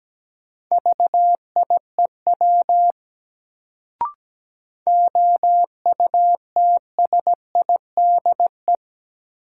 Morse code, heard as a keyed tone, about 17 wpm, 700 Hz. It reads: VIEW OUTSIDE
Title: Sentences from Top 900 Words - New Word Drill - 17wpm